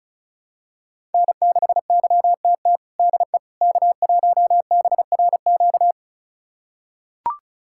Morse code, heard as a keyed tone, about 36 wpm, 700 Hz. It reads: N6YTT DE K1BRQ
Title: USA Callsign DE Callsign 35wpm